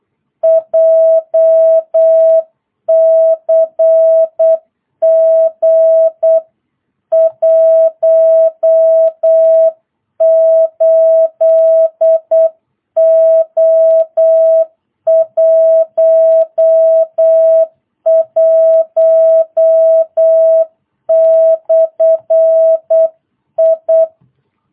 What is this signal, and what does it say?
Morse code, heard as a keyed tone, about 8 wpm, 660 Hz. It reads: JCG 18O11/I